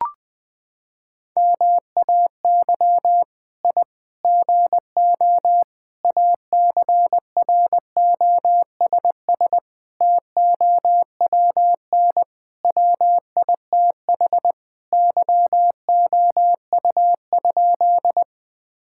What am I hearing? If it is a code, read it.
Morse code, heard as a keyed tone, about 20 wpm, 700 Hz. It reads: MAY I GO ACROSS TOWN WITH YOU?
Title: Sentences from Top 1000 Words - Review Part 5 - 20wpm